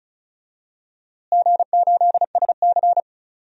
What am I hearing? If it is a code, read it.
Morse code, heard as a keyed tone, about 36 wpm, 700 Hz. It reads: G8SC
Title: DXCC-100 Call Signs - 4 Characters - Repeated 3x - 35wpm